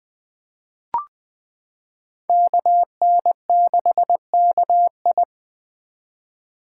Morse code, heard as a keyed tone, about 21 wpm, 700 Hz. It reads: KN6KI
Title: 2023 POTA Hunter Call Signs - 20wpm